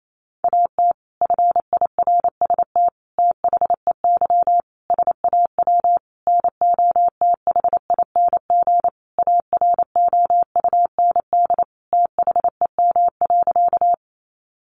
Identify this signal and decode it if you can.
Morse code, heard as a keyed tone, about 28 wpm, 700 Hz. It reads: AT FIRST THEY SAW NOTHING AROUND THEM.
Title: Sentences from Top 600 Words 28wpm